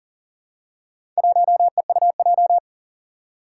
Morse code, heard as a keyed tone, about 40 wpm, 700 Hz. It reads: 1EUJ
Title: Four Challenging Letter-Number Combinations 40wpm